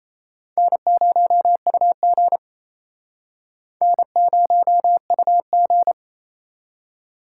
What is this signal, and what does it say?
Morse code, heard as a keyed tone, about 34 wpm, 700 Hz. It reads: N0UG N0UG
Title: Speed Racing - USA 1x2 Call Signs 28wpm